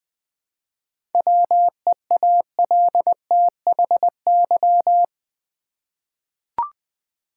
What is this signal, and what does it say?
Morse code, heard as a keyed tone, about 20 wpm, 700 Hz. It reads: WEALTHY
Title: 7 Letter Words - First 3 Letters Given - 20wpm